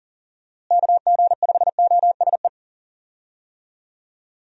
Morse code, heard as a keyed tone, about 37 wpm, 700 Hz. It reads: KG5OSE
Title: USA Call Signs - 6 Characters - Repeated 2x - 40wpm